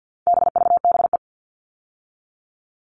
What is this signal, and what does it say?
Morse code, heard as a keyed tone, about 50 wpm, 700 Hz. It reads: BVBE
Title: All Cut Numbers - 4-Digit 50wpm